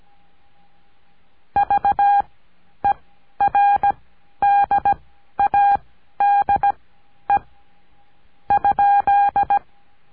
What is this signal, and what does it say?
Morse code, heard as a keyed tone, about 17 wpm, spaced as for 11 wpm, 810 Hz. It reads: VERDADE ?
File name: cursos / CURSO DE TELEGRAFIA / Telegrafia - Trilha 17